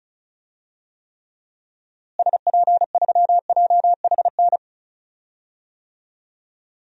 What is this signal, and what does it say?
Morse code, heard as a keyed tone, about 34 wpm, 700 Hz. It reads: SP3JHN
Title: DXCC-100 Call Signs - 6 Characters - Repeated 2x - 35wpm